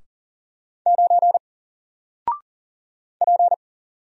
Morse code, heard as a keyed tone, about 40 wpm, 700 Hz. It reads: 9 P
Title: Single Character (Letter:Number:Punctuation:Prosign) Alt 40wpm